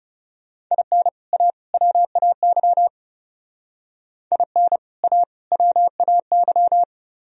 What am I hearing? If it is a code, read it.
Morse code, heard as a keyed tone, about 36 wpm, 700 Hz. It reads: IN A WAY IN A WAY
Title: Speed-Racing - Sets of 3 Words 30wpm